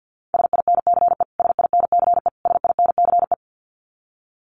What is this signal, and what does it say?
Morse code, heard as a keyed tone, about 49 wpm, 700 Hz. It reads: SINCE SINCE SINCE
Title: Top 500 Words Repeated 50wpm